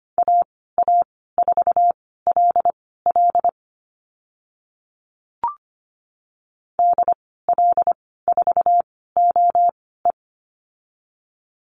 Morse code, heard as a keyed tone, about 25 wpm, spaced as for 16 wpm, 700 Hz. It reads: AA4LL DL4OE
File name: Lesson 15g - Number 5 - Call Signs 25wpm